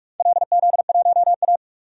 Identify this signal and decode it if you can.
Morse code, heard as a keyed tone, about 45 wpm, 700 Hz. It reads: LZ1A
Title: DXCC-100 Call Signs - 4 Characters - 45wpm